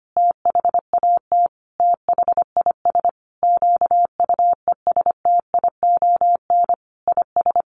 Morse code, heard as a keyed tone, about 25 wpm, 700 Hz. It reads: THAT THIS QUESTION IS
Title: Sentences from Top 1400 Words 25wpm